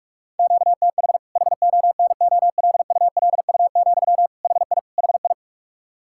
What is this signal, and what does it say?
Morse code, heard as a keyed tone, about 46 wpm, 700 Hz. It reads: QTH HONOLULU, HI HI
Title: Practice QSO - Exchange 1 - Beginning - 45wpm